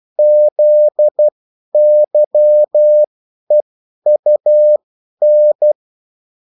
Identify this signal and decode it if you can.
Morse code, heard as a keyed tone, about 12 wpm, spaced as for 10 wpm, 600 Hz. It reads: ZYEUN
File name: morse mp3 / revision9-vitesse 10 mots